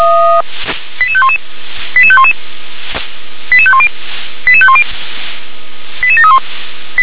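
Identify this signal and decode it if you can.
Five-tone selective call (ZVEI), heard as quick runs of five tones, tones of about 70 ms. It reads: 88511 88511 88510 88510 88731
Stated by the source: ZVEI five-tone call received by SDR.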